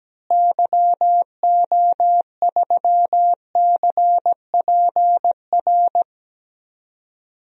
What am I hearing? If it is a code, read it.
Morse code, heard as a keyed tone, about 17 wpm, 700 Hz. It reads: YO3CPR
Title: DXCC-100 Call Signs - 6 Characters - Repeated 2x - 17wpm